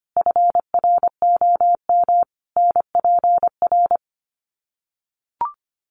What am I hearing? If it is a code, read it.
Morse code, heard as a keyed tone, about 25 wpm, 700 Hz. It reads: FROM NPR
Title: Sets of 5 Words 25wpm